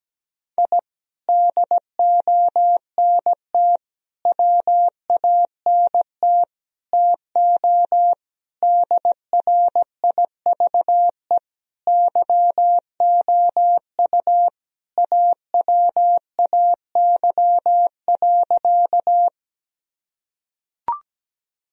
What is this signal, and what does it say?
Morse code, heard as a keyed tone, about 17 wpm, 700 Hz. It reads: I DONT WANT TO DRIVE YOU AWAY.
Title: Sentences from Top 1000 Words - Review Part 2 - 17wpm